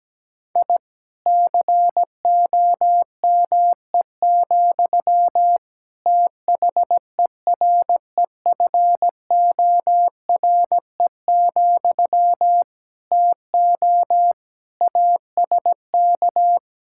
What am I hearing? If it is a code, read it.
Morse code, heard as a keyed tone, about 17 wpm, 700 Hz. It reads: I COME, THEREFORE, TO ASK
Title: Sentences from Top 1000 Words - Review Part 5 - 17wpm